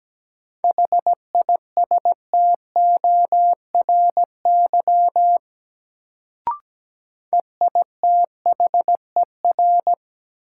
Morse code, heard as a keyed tone, about 17 wpm, 700 Hz. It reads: HISTORY EITHER
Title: Sentences from Top 600 Words - New Word Drill - 17wpm